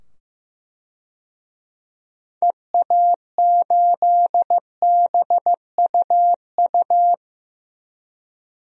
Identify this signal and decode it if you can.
Morse code, heard as a keyed tone, about 15 wpm, 700 Hz. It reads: EA8BUU
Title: DXCC-100 Call Signs - 6 Characters - 15wpm